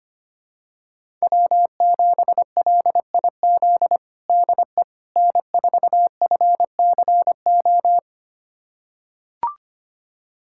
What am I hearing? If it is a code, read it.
Morse code, heard as a keyed tone, about 25 wpm, 700 Hz. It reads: W7LIZ DE N4FCO
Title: USA Callsign DE Callsign - 5 Characters - 25wpm